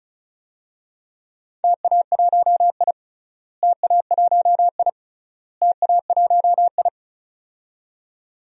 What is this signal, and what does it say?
Morse code, heard as a keyed tone, about 34 wpm, 700 Hz. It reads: TA1I TA1I TA1I
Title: DXCC-100 Call Signs - 4 Characters - Repeated 3x - 35wpm